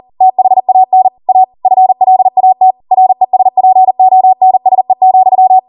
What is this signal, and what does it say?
Morse code, heard as a keyed tone, about 40 wpm, 750 Hz. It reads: THAN A FLAT RESPONSE,